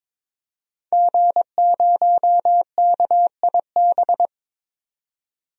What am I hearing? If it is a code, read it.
Morse code, heard as a keyed tone, about 22 wpm, 700 Hz. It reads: G0KIB
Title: DXCC-100 Call Signs - 5 Characters - 22wpm